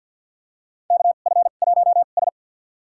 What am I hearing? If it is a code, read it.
Morse code, heard as a keyed tone, about 46 wpm, spaced as for 35 wpm, 700 Hz. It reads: KF1S
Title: USA Call Signs - 2x Character Spacing - 50wpm